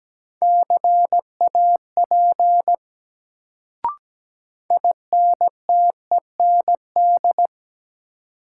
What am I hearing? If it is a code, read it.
Morse code, heard as a keyed tone, about 17 wpm, 700 Hz. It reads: CAP INTEND
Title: Sentences from Top 1600 Words - New Word Drill Alt - 17wpm